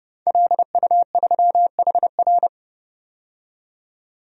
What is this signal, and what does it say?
Morse code, heard as a keyed tone, about 32 wpm, 700 Hz. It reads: LU3HR
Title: DXCC-100 Call Signs - 5 Characters - 30wpm